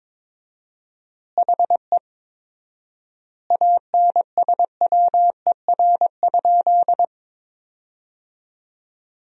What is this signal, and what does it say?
Morse code, heard as a keyed tone, about 22 wpm, 700 Hz. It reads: HE ANSWER?
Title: 4x Word-Spacing - Sentences from Top 300 Words - 22wpm